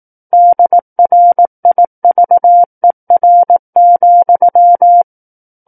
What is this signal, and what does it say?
Morse code, heard as a keyed tone, about 18 wpm, 700 Hz. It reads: DRIVER,
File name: Book One / 09-Chapter 9 - 18 WPM